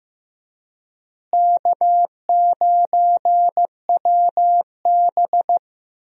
Morse code, heard as a keyed tone, about 15 wpm, 700 Hz. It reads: K9WB
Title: USA Call Signs - 4 Characters - Repeated 2x - 15wpm